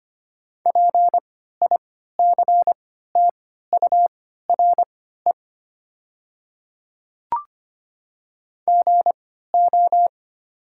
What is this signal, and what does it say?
Morse code, heard as a keyed tone, about 25 wpm, spaced as for 14 wpm, 700 Hz. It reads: PICTURE GO
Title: 3x Character Spacing - Top 200 Words - 25wpm